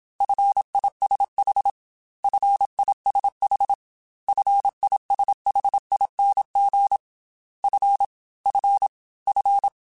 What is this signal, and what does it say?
Morse code, heard as a keyed tone, about 27 wpm, 800 Hz. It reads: FISH FISH FISHING F F F
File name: Single F mp3s / Single F Prefixes 3x2 Twice @25